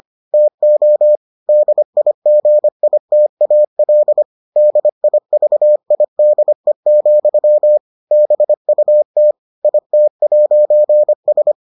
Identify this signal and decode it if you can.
Morse code, heard as a keyed tone, about 25 wpm, 600 Hz. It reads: TO DIGITAL DIVIDE, BUT IT'S